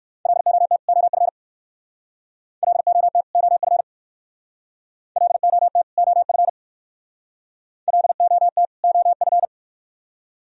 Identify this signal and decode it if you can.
Morse code, heard as a keyed tone, about 60 wpm, 700 Hz. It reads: LOT OF LOT OF LOT OF LOT OF